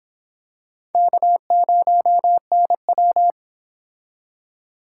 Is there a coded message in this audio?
Yes: Morse code from 0.9 to 3.3 s, heard as a keyed tone, about 26 wpm, 700 Hz. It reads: K0NW